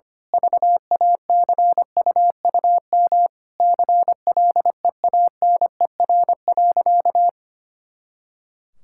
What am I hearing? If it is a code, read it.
Morse code, heard as a keyed tone, about 25 wpm, 700 Hz. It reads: VACUUM CLEANER.
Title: Pangram Challenge v2 - 25wpm